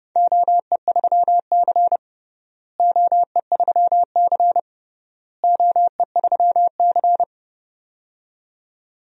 Morse code, heard as a keyed tone, about 29 wpm, 700 Hz. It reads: OE3C OE3C OE3C
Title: DXCC-100 Call Signs - 4 Characters - Repeated 3x - 30wpm